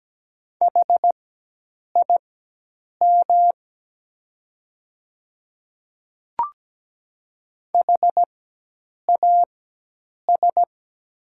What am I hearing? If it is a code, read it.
Morse code, heard as a keyed tone, about 17 wpm, spaced as for 8 wpm, 700 Hz. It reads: HIM HAS
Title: 4x Character Spacing - Top 300 Words - 17wpm